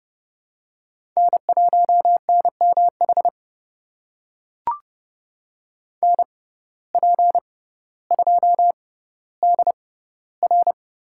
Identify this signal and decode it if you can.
Morse code, heard as a keyed tone, about 29 wpm, spaced as for 15 wpm, 700 Hz. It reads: N1NMH NP2DR
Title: USA Call Signs - 5 Characters - 6x Character Spacing - 30wpm